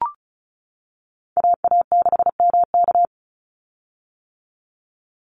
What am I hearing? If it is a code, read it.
Morse code, heard as a keyed tone, about 34 wpm, 700 Hz. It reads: AA6MK
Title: Call Signs - 35wpm